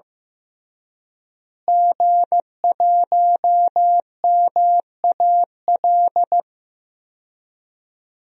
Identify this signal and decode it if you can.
Morse code, heard as a keyed tone, about 15 wpm, 700 Hz. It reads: G1MAL
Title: DXCC-100 Call Signs - 5 Characters - Repeated 3x - 15wpm